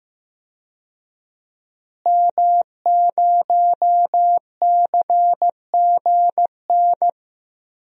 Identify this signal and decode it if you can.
Morse code, heard as a keyed tone, about 15 wpm, 700 Hz. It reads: M0CGN
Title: DXCC-100 Call Signs - 5 Characters - Repeated 3x - 15wpm